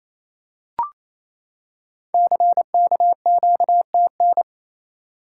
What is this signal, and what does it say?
Morse code, heard as a keyed tone, about 29 wpm, 700 Hz. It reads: CKQTN
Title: Code Groups - 5 Letters - 28wpm